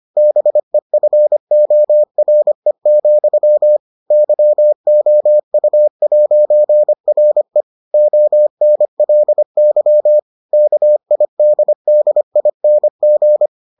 Morse code, heard as a keyed tone, about 25 wpm, 600 Hz. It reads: BEFORE, YOU'RE ONLY KIDDING